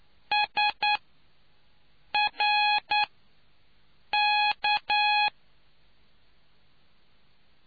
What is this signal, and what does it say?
Morse code, heard as a keyed tone, about 9 wpm, spaced as for 5 wpm, 840 Hz. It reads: SRK